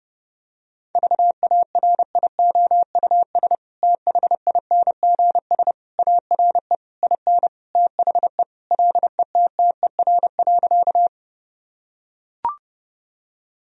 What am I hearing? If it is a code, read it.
Morse code, heard as a keyed tone, about 30 wpm, 700 Hz. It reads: VARIOUS THINGS ARE IN THE LETTER.
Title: Sentences from Top 1000 Words - Review Part 4 - 30wpm